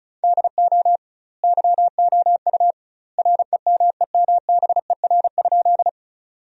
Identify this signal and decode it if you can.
Morse code, heard as a keyed tone, about 36 wpm, 700 Hz. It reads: DO YOU REMEMBER?
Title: 2x Word-Spacing - Sentences from Top 400 Words - 35wpm